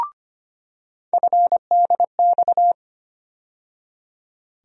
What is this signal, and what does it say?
Morse code, heard as a keyed tone, about 25 wpm, 700 Hz. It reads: FDX
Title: USA Call Sign Postfixes - 3 Letters 25wpm